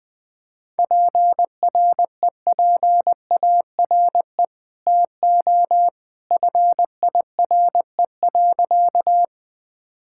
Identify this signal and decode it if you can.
Morse code, heard as a keyed tone, about 20 wpm, 700 Hz. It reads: PREPARE TO FIRE.
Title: Sentences from Top 1000 Words - Review Part 6 - 20wpm